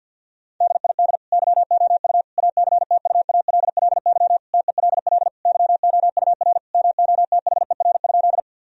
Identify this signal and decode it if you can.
Morse code, heard as a keyed tone, about 50 wpm, 700 Hz. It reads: DID YOU ACTUALLY TELL YOUR MOTHER?
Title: Sentences from Top 1000 Words - Review Part 4 - 50wpm